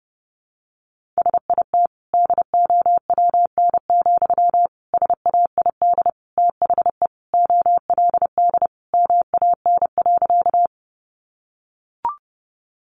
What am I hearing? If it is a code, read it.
Morse code, heard as a keyed tone, about 30 wpm, 700 Hz. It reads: SIT DOWN, SAID THE OLD MAN.
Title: Sentences from Top 500 Words 30wpm